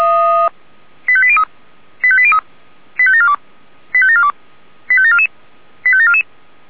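Five-tone selective call (ZVEI), heard as quick runs of five tones, tones of about 70 ms. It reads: 85803 85803 85742 85742 85744 85744
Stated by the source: ZVEI five-tone selective call 85730 received on SDR.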